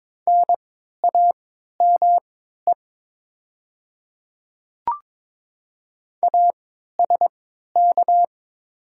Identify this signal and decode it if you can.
Morse code, heard as a keyed tone, about 22 wpm, spaced as for 13 wpm, 700 Hz. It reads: NAME ASK
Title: 3x Character Spacing - Top 200 Words - 22wpm